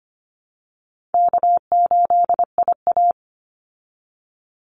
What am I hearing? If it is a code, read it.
Morse code, heard as a keyed tone, about 25 wpm, 700 Hz. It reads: K8IA
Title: Easy Call Signs Repeated 3x - 25wpm